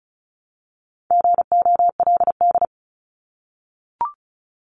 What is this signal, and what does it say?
Morse code, heard as a keyed tone, about 34 wpm, 700 Hz. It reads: GOLD
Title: Top 500 Words 35wpm